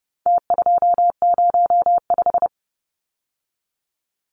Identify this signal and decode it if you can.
Morse code, heard as a keyed tone, about 30 wpm, 700 Hz. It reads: T205
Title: Numbers - Contest Style - 4-Digit 30wpm